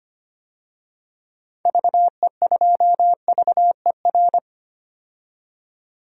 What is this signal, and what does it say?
Morse code, heard as a keyed tone, about 25 wpm, 700 Hz. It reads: VE2VER
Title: DXCC-100 Call Signs - 6 Characters - 25wpm